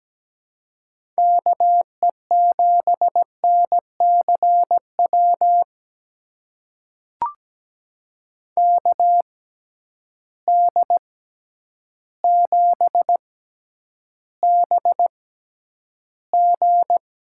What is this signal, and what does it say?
Morse code, heard as a keyed tone, about 17 wpm, spaced as for 9 wpm, 700 Hz. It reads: KE7NCW KD7BG
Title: USA Call Signs - 6 Characters - 6x Character Spacing - 17wpm